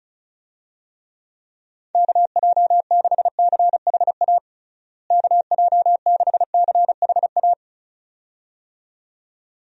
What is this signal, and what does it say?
Morse code, heard as a keyed tone, about 36 wpm, 700 Hz. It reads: KJ6CHA KJ6CHA
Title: USA Call Signs - 6 Characters - Repeated 2x - 35wpm